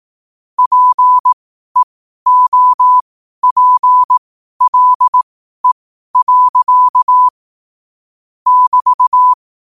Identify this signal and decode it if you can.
Morse code, heard as a keyed tone, about 18 wpm, spaced as for 12 wpm, 1000 Hz. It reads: PEOPLE. <BT>
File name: qotdfiles / 12wpm